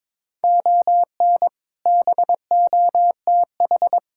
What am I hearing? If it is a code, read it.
Morse code, heard as a keyed tone, about 22 wpm, 700 Hz. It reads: ON BOTH
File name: practice / 22wpm / Sets of 6 Words - 22wpm